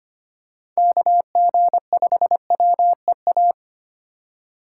Morse code, heard as a keyed tone, about 25 wpm, 700 Hz. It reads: KG5WEA
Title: USA Call Signs - 6 Characters - 3x Character Spacing - 25wpm